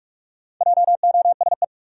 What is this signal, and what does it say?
Morse code, heard as a keyed tone, about 47 wpm, 700 Hz. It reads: JOSE
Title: USA Common Names - Max 5 Letters - 45wpm